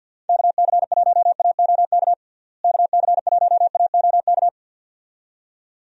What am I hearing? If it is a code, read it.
Morse code, heard as a keyed tone, about 49 wpm, 700 Hz. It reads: KC1AOK KC1AOK
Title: USA Call Signs - 6 Characters - Repeated 2x - 50wpm